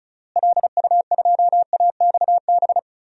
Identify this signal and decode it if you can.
Morse code, heard as a keyed tone, about 34 wpm, 700 Hz. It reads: LU2AXB
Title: DXCC-100 Call Signs - 6 Characters - Repeated 3x - 35wpm